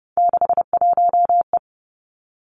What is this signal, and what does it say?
Morse code, heard as a keyed tone, about 30 wpm, 700 Hz. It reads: 61E
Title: Three Challenging Letter-Number Combinations 30wpm